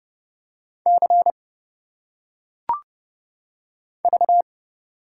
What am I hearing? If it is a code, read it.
Morse code, heard as a keyed tone, about 31 wpm, 700 Hz. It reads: C V
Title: Single Letters - Phonetic Pronunciation 30wpm